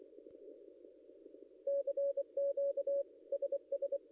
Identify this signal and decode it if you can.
Morse code, heard as a keyed tone, about 25 wpm, 560 Hz. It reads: CQ SS
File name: contests / ss-cw-k5tr-op-n5rz -0800